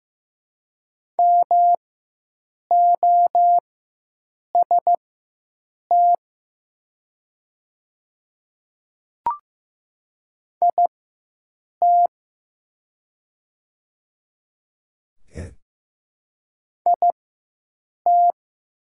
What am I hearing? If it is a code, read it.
Morse code, heard as a keyed tone, about 15 wpm, spaced as for 7 wpm, 700 Hz. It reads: MOST IT IT
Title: 4x Character Spacing - Top 100 Words - 15wpm